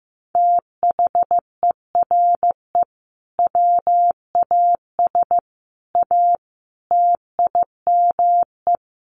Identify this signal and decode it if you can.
Morse code, heard as a keyed tone, about 15 wpm, 700 Hz. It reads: THERE WAS A TIME
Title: Sets of 5 Words 15wpm